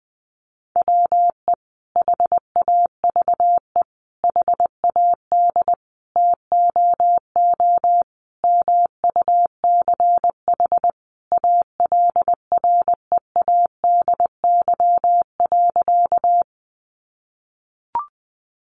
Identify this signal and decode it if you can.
Morse code, heard as a keyed tone, about 20 wpm, 700 Hz. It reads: WE HAVE HAD TOO MUCH ALREADY.
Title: Sentences from Top 700 Words 20wpm